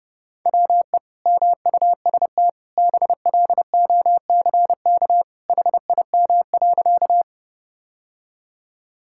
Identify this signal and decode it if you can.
Morse code, heard as a keyed tone, about 30 wpm, 700 Hz. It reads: WE MUST BLOCK HIM.
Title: Sentences from Top 1600 Words 30wpm